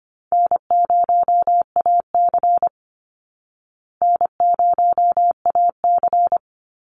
Morse code, heard as a keyed tone, about 25 wpm, 700 Hz. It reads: N0AC N0AC
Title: Easy Call Signs Repeated 3x - 25wpm